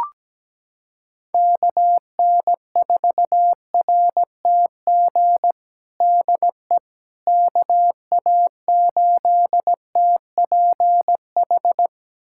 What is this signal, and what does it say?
Morse code, heard as a keyed tone, about 17 wpm, 700 Hz. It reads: KN4RTG DE KA8TPH
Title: USA Callsign DE Callsign - 6 Characters - 17wpm